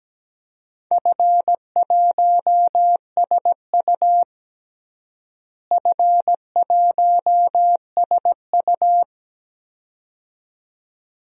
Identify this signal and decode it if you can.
Morse code, heard as a keyed tone, about 17 wpm, 700 Hz. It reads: F1SU F1SU
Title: DXCC-100 Call Signs - 4 Characters - Repeated 2x - 17wpm